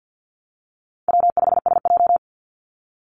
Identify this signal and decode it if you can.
Morse code, heard as a keyed tone, about 49 wpm, 700 Hz. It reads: W5SJ
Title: Call Signs Repeated 3x - 50wpm